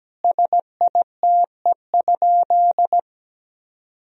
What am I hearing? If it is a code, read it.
Morse code, heard as a keyed tone, about 17 wpm, 700 Hz. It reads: SITE?
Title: Sentences from Top 1000 Words - Review Part 4 - 17wpm